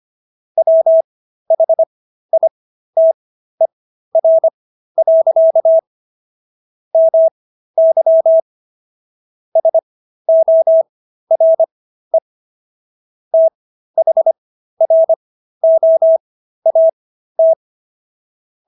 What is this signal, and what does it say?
Morse code, heard as a keyed tone, about 25 wpm, spaced as for 13 wpm, 650 Hz. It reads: WHITER. MY SORE THROAT